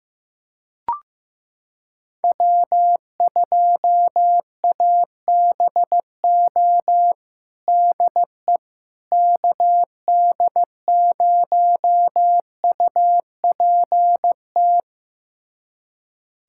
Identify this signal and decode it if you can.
Morse code, heard as a keyed tone, about 15 wpm, 700 Hz. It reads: W2ABO DE KD0UPT
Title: USA Callsign DE Callsign 15wpm